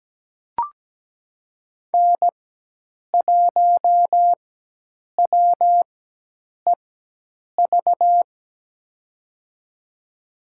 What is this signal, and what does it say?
Morse code, heard as a keyed tone, about 17 wpm, spaced as for 8 wpm, 700 Hz. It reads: N1WEV
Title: USA Call Signs - 5 Characters - 4x Character Spacing - 17wpm